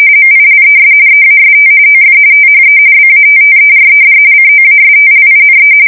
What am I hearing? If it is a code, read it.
RTTY, about 45.45 baud. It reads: MILES EAST OF THE BATTERED COASTLINE
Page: 170 Shift 2125/2295